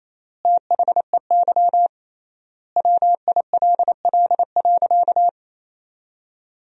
Morse code, heard as a keyed tone, about 28 wpm, 700 Hz. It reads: THEY WILL.
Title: 3x Word-Spacing - Sentences from Top 500 Words - 28wpm